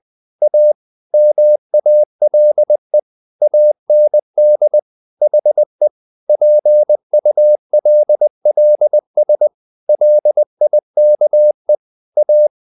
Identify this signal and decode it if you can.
Morse code, heard as a keyed tone, about 20 wpm, 600 Hz. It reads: A MALE AND HE PULLS LIKE A
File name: the old man and the sea / the old man and the sea part 1 20wpm